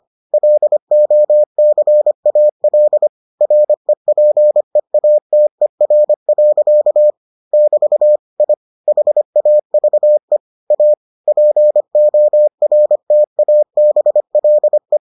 Morse code, heard as a keyed tone, about 25 wpm, 600 Hz. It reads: LOCAL REPEATER. <BT> I HAVE A PORTABLE